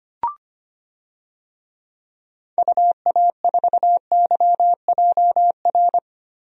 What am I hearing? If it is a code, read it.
Morse code, heard as a keyed tone, about 25 wpm, 700 Hz. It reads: UA4YJR
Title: DXCC-100 Call Signs - 6 Characters - Repeated 3x - 25wpm